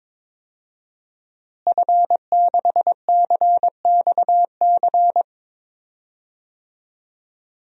Morse code, heard as a keyed tone, about 22 wpm, 700 Hz. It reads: F6CXC
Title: DXCC-100 Call Signs - 5 Characters - Repeated 2x - 22wpm